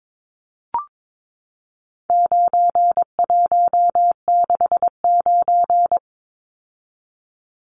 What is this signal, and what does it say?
Morse code, heard as a keyed tone, about 22 wpm, 700 Hz. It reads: 9169
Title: Numbers - 4-Digit 22wpm